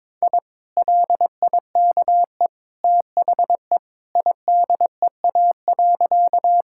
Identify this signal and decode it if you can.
Morse code, heard as a keyed tone, about 22 wpm, 700 Hz. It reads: I LIKE THE IDEA.
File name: practice / 22wpm / Sentences with 2-Word Phrases - 22wpm